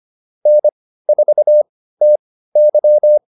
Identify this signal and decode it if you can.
Morse code, heard as a keyed tone, about 25 wpm, spaced as for 15 wpm, 600 Hz. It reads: N4TY